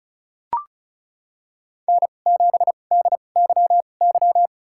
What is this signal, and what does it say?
Morse code, heard as a keyed tone, about 36 wpm, spaced as for 25 wpm, 700 Hz. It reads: N7DYY
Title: USA Call Signs - 5 Characters - 2x Character Spacing - 35wpm